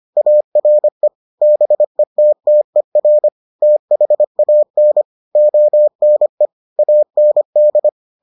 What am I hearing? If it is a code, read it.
Morse code, heard as a keyed tone, about 25 wpm, 600 Hz. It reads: ARE BETTER THAN ONE AND